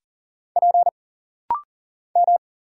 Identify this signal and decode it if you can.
Morse code, heard as a keyed tone, about 40 wpm, 700 Hz. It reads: P M
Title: Single Letter-Number Alt - Rapid-Fire 40wpm